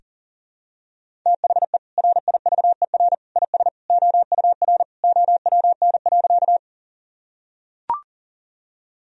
Morse code, heard as a keyed tone, about 40 wpm, 700 Hz. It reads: THE RIVER IS OUR OWN.
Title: Sentences from Top 1000 Words - Review Part 1 - 40wpm